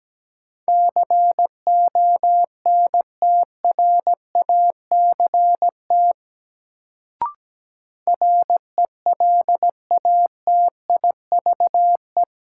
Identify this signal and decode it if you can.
Morse code, heard as a keyed tone, about 17 wpm, 700 Hz. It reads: CONTRACT RELATIVE
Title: Sentences from Top 1400 Words - New Word Drill Alt - 17wpm